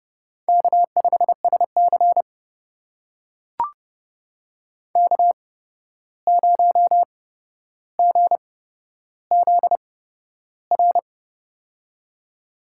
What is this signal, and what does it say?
Morse code, heard as a keyed tone, about 32 wpm, spaced as for 12 wpm, 700 Hz. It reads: K5SC K0GZR
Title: USA Call Signs - 8x Character Spacing - 30wpm